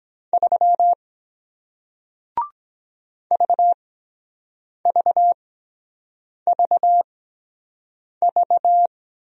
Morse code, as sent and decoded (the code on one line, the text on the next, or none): ...-- / ...- ...- ...- ...-
3 VVVV